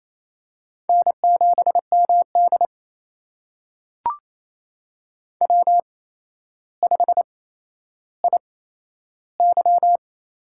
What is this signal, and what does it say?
Morse code, heard as a keyed tone, about 28 wpm, spaced as for 12 wpm, 700 Hz. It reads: N7MD W5IY
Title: USA Call Signs - 4 Characters - 8x Character Spacing - 28wpm